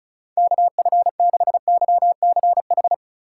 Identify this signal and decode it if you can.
Morse code, heard as a keyed tone, about 47 wpm, spaced as for 41 wpm, 700 Hz. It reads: KF6YCH
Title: Speed Racing - USA Call Signs 35wpm